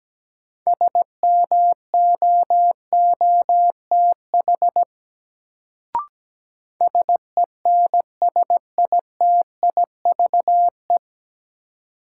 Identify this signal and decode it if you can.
Morse code, heard as a keyed tone, about 17 wpm, 700 Hz. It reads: SMOOTH SENSITIVE
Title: Sentences from Top 1400 Words - New Word Drill Alt - 17wpm